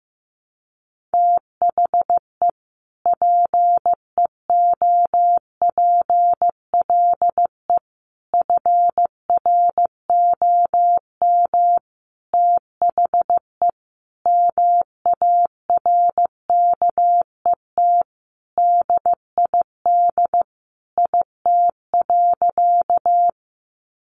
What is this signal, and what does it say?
Morse code, heard as a keyed tone, about 15 wpm, 700 Hz. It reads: THE PEOPLE FROM THE MARKET DID IT.
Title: Sentences from Top 600 Words 15wpm